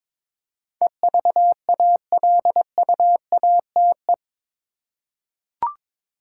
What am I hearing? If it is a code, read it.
Morse code, heard as a keyed tone, about 22 wpm, 700 Hz. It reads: EVALUATE
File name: practice / 22wpm / Eight Letter Words - 22wpm